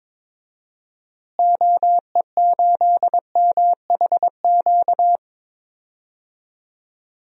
Morse code, heard as a keyed tone, about 22 wpm, 700 Hz. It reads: OE8MHQ
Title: DXCC-100 Call Signs - 6 Characters - Repeated 2x - 22wpm